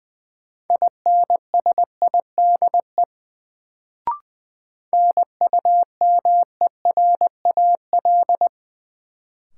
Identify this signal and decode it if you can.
Morse code, heard as a keyed tone, about 20 wpm, 700 Hz. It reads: INSIDE NUMERAL
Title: Sentences from Top 1000 Words - Review Part 2 - New Word Drill - 20wpm